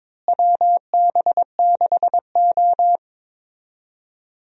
Morse code, heard as a keyed tone, about 22 wpm, 700 Hz. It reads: WB6O
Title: USA Call Signs - 4 Characters - Repeated 2x - 22wpm